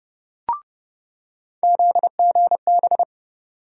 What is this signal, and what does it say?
Morse code, heard as a keyed tone, about 31 wpm, 700 Hz. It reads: ZGB
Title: USA Call Sign Postfixes - 3 Letters 30wpm